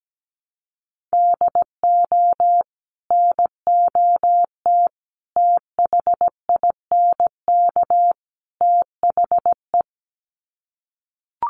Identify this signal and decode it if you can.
Morse code, heard as a keyed tone, about 17 wpm, 700 Hz. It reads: DO NOT THINK THE
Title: Sets of 4 Words 17wpm